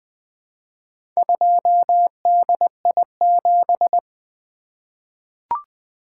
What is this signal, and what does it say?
Morse code, heard as a keyed tone, about 20 wpm, 700 Hz. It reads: 2DI7
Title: Four Letter-Number Combinations - 20wpm